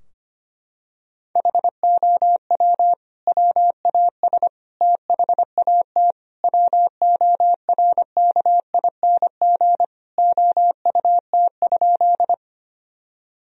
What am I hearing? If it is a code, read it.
Morse code, heard as a keyed tone, about 25 wpm, 700 Hz. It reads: HOW WAS THAT WORKING OUT?
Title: Sentences from Top 1000 Words - Review Part 4 - 25wpm